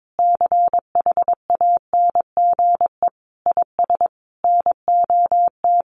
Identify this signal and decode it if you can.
Morse code, heard as a keyed tone, about 22 wpm, 700 Hz. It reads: CHANGE IS NOT
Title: Sentences from Top 1500 Words 22wpm